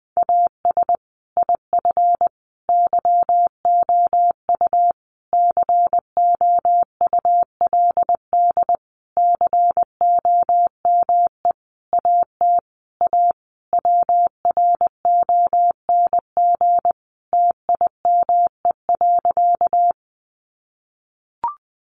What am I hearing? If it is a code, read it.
Morse code, heard as a keyed tone, about 20 wpm, 700 Hz. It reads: AS IF YOU COULD COME AT A WRONG TIME.
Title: Sentences from Top 900 Words 20wpm